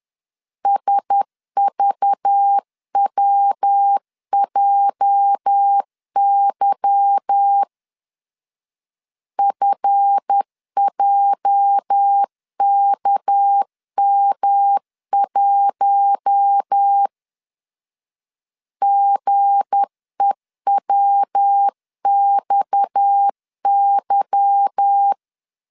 Morse code, heard as a keyed tone, about 10 wpm, 800 Hz. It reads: SVWJY FJKM1 GEWXY
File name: LEZIONI / 8 YJ45 / 8 YJ 10wpm 2651